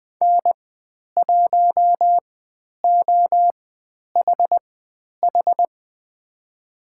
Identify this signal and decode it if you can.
Morse code, heard as a keyed tone, about 20 wpm, spaced as for 10 wpm, 700 Hz. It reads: N1OHH
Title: 102 Lesson 17g - Letter u - Call Signs 20wpm